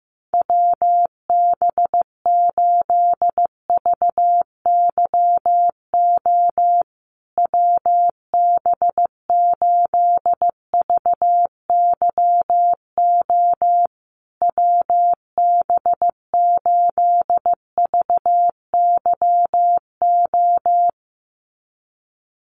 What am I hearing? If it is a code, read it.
Morse code, heard as a keyed tone, about 14 wpm, 700 Hz. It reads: WB8VYO WB8VYO WB8VYO
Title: USA Call Signs Repeated 3x 15wpm